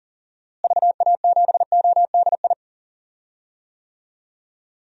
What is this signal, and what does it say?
Morse code, heard as a keyed tone, about 40 wpm, 700 Hz. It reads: VA7ODI
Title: DXCC-100 Call Signs - 6 Characters - Repeated 2x - 40wpm